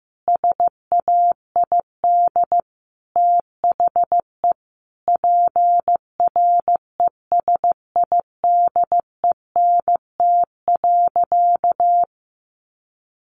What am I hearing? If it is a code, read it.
Morse code, heard as a keyed tone, about 15 wpm, 700 Hz. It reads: SAID THE PRESIDENT.
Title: Sentences from Top 1000 Words 15wpm